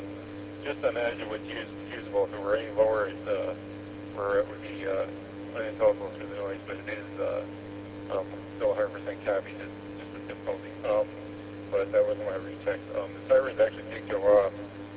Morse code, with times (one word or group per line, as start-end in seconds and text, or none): none